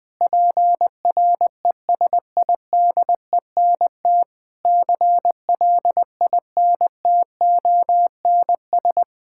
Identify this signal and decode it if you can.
Morse code, heard as a keyed tone, about 21 wpm, 700 Hz. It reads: PRESIDENT CLINTONS
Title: Pangram Challenge v2 - 20wpm